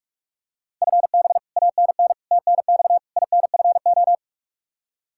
Morse code, heard as a keyed tone, about 44 wpm, 700 Hz. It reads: FB ANN TNX INFO
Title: Practice QSO - Exchange 3 - More Info - 45wpm